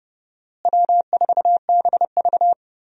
Morse code, heard as a keyed tone, about 30 wpm, 700 Hz. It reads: W4BV
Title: USA Call Signs - 4 Characters - Repeated 2x - 30wpm